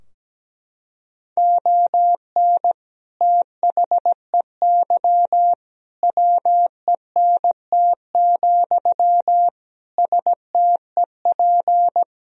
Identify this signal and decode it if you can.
Morse code, heard as a keyed tone, about 17 wpm, 700 Hz. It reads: ON THEY WENT, STEP
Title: Sentences from Top 1000 Words - Review Part 2 - 17wpm